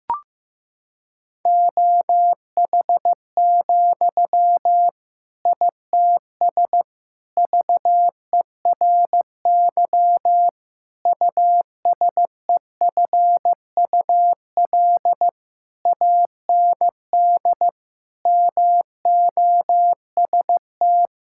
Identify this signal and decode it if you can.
Morse code, heard as a keyed tone, about 16 wpm, 700 Hz. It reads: OH, ITS VERY USEFUL AND MOST
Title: Sentences from Top 1000 Words - Review Part 6 - 15wpm